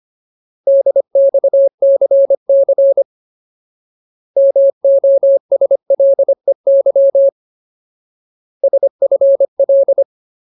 Morse code, heard as a keyed tone, about 25 wpm, 550 Hz. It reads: DXCC MOSLEY SFL
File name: Ultimate Word List / 25WPM Ultimate